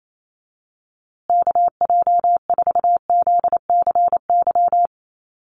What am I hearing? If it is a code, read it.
Morse code, heard as a keyed tone, about 28 wpm, 700 Hz. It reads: KJ4ZCY
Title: USA Call Signs Repeated 2x 28wpm